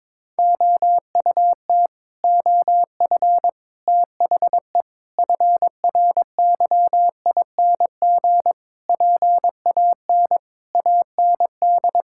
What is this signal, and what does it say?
Morse code, heard as a keyed tone, about 22 wpm, 700 Hz. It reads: OUT OF THE FRYING PAN AND
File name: Sets of 9 Words - 22wpm